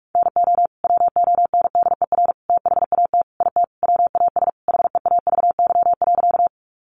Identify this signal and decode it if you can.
Morse code, heard as a keyed tone, about 44 wpm, 700 Hz. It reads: NO WONDER THAT IT WAS HEAVY.
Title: Sentences from Top 1100 Words 45wpm